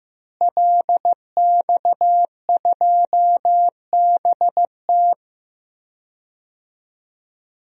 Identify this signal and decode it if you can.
Morse code, heard as a keyed tone, about 15 wpm, 700 Hz. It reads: LX2BT
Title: DXCC-100 Call Signs - 5 Characters - Repeated 2x - 15wpm